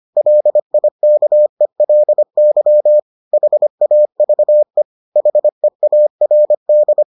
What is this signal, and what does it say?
Morse code, heard as a keyed tone, about 25 wpm, 600 Hz. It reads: LIKELY HAVE HEARD